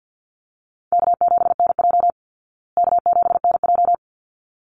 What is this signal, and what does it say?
Morse code, heard as a keyed tone, about 48 wpm, 700 Hz. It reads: K7NJ K7NJ
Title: Call Signs Repeated 3x - 50wpm